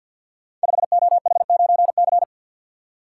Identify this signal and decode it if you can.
Morse code, heard as a keyed tone, about 47 wpm, 700 Hz. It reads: 5OH9C
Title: Code Groups - 5 Letter:Numbers - 50wpm